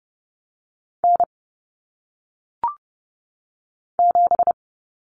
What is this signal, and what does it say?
Morse code, heard as a keyed tone, about 30 wpm, 700 Hz. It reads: N 7
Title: Common Cut Numbers - 1-Digit 30wpm